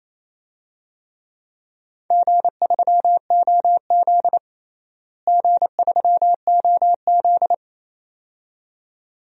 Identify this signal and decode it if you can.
Morse code, heard as a keyed tone, about 28 wpm, 700 Hz. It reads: G3OZ G3OZ
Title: DXCC-100 Call Signs - 4 Characters - Repeated 2x - 28wpm